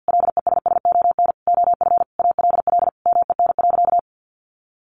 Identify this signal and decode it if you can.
Morse code, heard as a keyed tone, about 50 wpm, 700 Hz. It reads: LESSON OF ALL MEN.
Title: Sentences from Top 1200 Words 50wpm